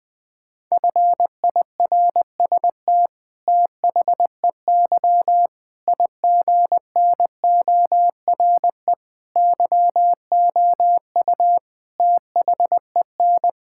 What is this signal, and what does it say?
Morse code, heard as a keyed tone, about 20 wpm, 700 Hz. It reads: FIRST THEY IGNORE YOU THEN
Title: Sentences with 8-Word Phrases - Spoken Beginning - 20wpm